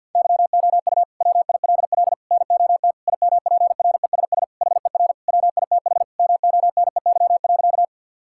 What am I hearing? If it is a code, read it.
Morse code, heard as a keyed tone, about 52 wpm, 700 Hz. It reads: YOU WILL NOT IMPRESS HER WITH MONEY.
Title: Sentences from Top 1600 Words 50wpm